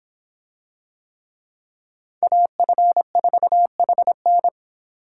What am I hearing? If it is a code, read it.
Morse code, heard as a keyed tone, about 26 wpm, 700 Hz. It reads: AF4HN